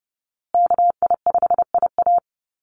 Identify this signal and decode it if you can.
Morse code, heard as a keyed tone, about 29 wpm, 700 Hz. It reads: KI5IA
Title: USA Call Signs Repeated 3x 30wpm